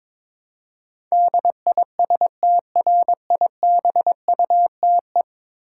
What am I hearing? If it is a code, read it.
Morse code, heard as a keyed tone, about 22 wpm, 700 Hz. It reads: DISTRIBUTE
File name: practice / 22wpm / 10 Letter Words - First 3 Letters Given - 22wpm